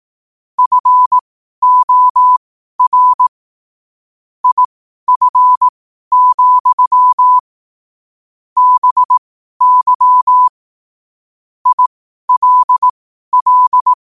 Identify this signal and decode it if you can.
Morse code, heard as a keyed tone, about 18 wpm, spaced as for 12 wpm, 1000 Hz. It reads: FOR IF, BY ILL
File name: qotdfiles / 12wpm